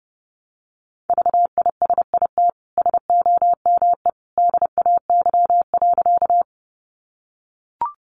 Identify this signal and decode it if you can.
Morse code, heard as a keyed tone, about 30 wpm, 700 Hz. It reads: VISIT SOME DAY.
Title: Sentences from Top 900 Words 30wpm